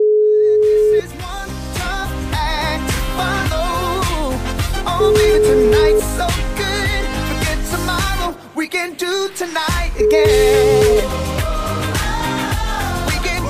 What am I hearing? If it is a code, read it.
Ringback tone (Europe, 425 Hz), with noise.